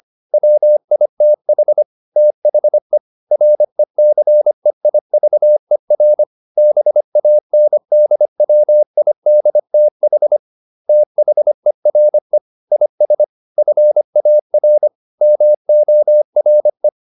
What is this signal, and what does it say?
Morse code, heard as a keyed tone, about 25 wpm, 600 Hz. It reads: WITH THE RECEIVER BANDWIDTH THERE IS FAR MORE